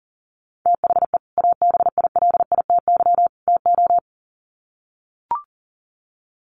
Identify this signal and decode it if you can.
Morse code, heard as a keyed tone, about 40 wpm, 700 Hz. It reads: THE ABILITY TO
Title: Sets of 3 Words 40wpm